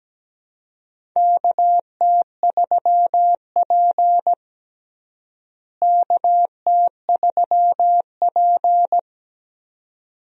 Morse code, heard as a keyed tone, about 17 wpm, 700 Hz. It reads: KT3P KT3P
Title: USA Call Signs - 4 Characters - Repeated 2x - 17wpm